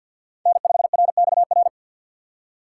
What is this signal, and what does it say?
Morse code, heard as a keyed tone, about 53 wpm, 700 Hz. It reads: N5RXR
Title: USA Call Signs - 5 Characters - 8x Character Spacing - 50wpm